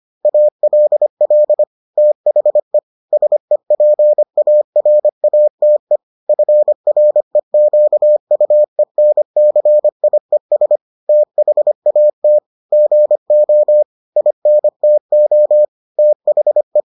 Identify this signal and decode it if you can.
Morse code, heard as a keyed tone, about 25 wpm, 600 Hz. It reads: ALL THE SEPARATE FREQUENCIES THAT GO INTO THE